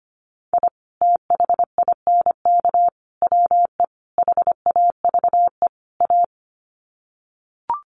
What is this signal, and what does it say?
Morse code, heard as a keyed tone, about 25 wpm, 700 Hz. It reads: I THINK WE HAVE A
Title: Sets of 5 Words Encore 25wpm